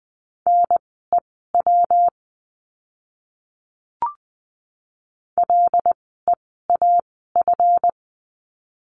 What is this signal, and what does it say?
Morse code, heard as a keyed tone, about 20 wpm, spaced as for 14 wpm, 700 Hz. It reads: NEW LEAF
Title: 139 Lesson 23f - Letter f - Words 20wpm